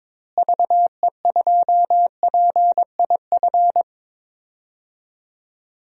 Morse code, heard as a keyed tone, about 22 wpm, 700 Hz. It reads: VE2PIF
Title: DXCC-100 Call Signs - 6 Characters - Repeated 2x - 22wpm